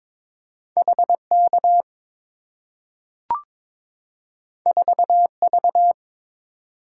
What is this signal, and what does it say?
Morse code, heard as a keyed tone, about 22 wpm, 690 Hz. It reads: HK 4V
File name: practice / 22wpm / Two Letter-Number Combinations - 22wpm